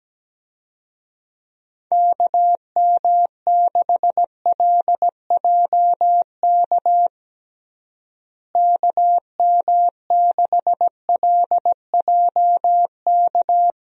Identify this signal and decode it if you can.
Morse code, heard as a keyed tone, about 17 wpm, 700 Hz. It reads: KM6LJK KM6LJK
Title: USA Call Signs - 6 Characters - Repeated 2x - 17wpm